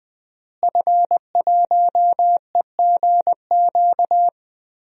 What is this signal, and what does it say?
Morse code, heard as a keyed tone, about 21 wpm, 700 Hz. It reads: F1EGQ
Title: DXCC-100 Call Signs - 5 Characters - Repeated 2x - 20wpm